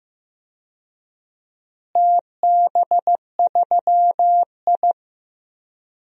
Morse code, heard as a keyed tone, about 15 wpm, 700 Hz. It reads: TB3I
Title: DXCC-100 Call Signs - 4 Characters - Repeated 3x - 15wpm